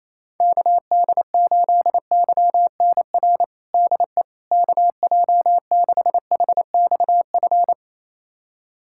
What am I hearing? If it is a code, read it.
Morse code, heard as a keyed tone, about 28 wpm, 700 Hz. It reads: KD8YNR DE KJ6HXF
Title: USA Callsign DE Callsign 28wpm